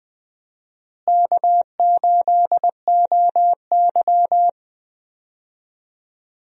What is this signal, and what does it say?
Morse code, heard as a keyed tone, about 20 wpm, 700 Hz. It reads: K8OY
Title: USA Call Signs - 4 Characters - Repeated 3x - 20wpm